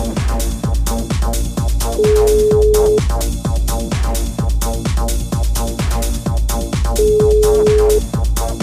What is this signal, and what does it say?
Ringback tone (Europe, 425 Hz), with noise.